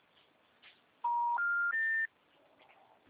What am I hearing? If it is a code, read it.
Special information tone.